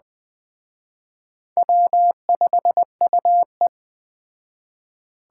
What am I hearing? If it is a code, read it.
Morse code, heard as a keyed tone, about 21 wpm, 700 Hz. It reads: W5UE